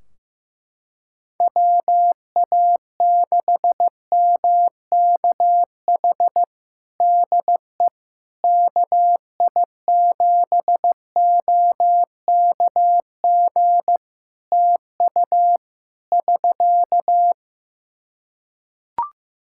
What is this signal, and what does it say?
Morse code, heard as a keyed tone, about 15 wpm, 700 Hz. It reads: WA6MKH DE KI7OKG TU <SK>
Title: Practice QSO - Exchange 4 - Ending - 15wpm